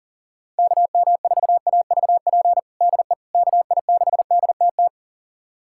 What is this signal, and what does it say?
Morse code, heard as a keyed tone, about 40 wpm, 700 Hz. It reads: KM4AVP DE KI6DTT
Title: USA Callsign DE Callsign - 6 Characters - 40wpm